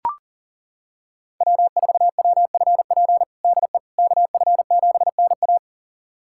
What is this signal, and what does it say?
Morse code, heard as a keyed tone, about 37 wpm, 700 Hz. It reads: W4WFP DE KF7NA
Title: USA Callsign DE Callsign - 5 Characters - 40wpm